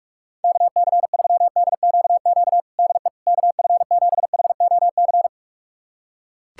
Morse code, heard as a keyed tone, about 41 wpm, 700 Hz. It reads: KC3DQQ DE KF7HOC
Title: USA Callsign DE Callsign 45wpm